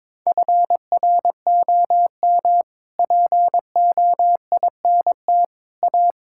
Morse code, heard as a keyed tone, about 22 wpm, 700 Hz. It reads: FROM POINT A
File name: practice / 22wpm / Sets of 6 Words - 22wpm